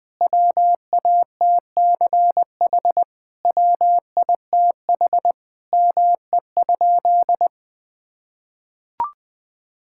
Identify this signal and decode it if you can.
Morse code, heard as a keyed tone, about 21 wpm, 700 Hz. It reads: WATCH WITH ME?
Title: Sentences from Top 1000 Words - Review Part 5 - 20wpm